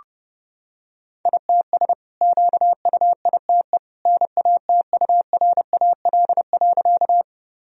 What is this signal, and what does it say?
Morse code, heard as a keyed tone, about 30 wpm, 700 Hz. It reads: ITS QUITE NATURAL.
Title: Sentences from Top 1000 Words - Review Part 4 - 30wpm